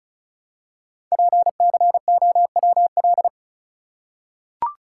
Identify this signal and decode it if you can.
Morse code, heard as a keyed tone, about 34 wpm, 690 Hz. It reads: PCOWL
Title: Code Groups - 5 Letters - 35wpm